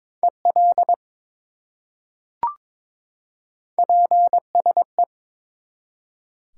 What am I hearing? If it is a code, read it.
Morse code, heard as a keyed tone, about 23 wpm, 700 Hz. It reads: EL PSE
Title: Top 180 QSO Elements 22wpm